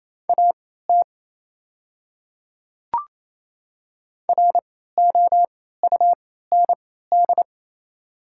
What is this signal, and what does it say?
Morse code, heard as a keyed tone, about 28 wpm, spaced as for 16 wpm, 700 Hz. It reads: AT ROUND
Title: 3x Character Spacing - Top 300 Words - 28wpm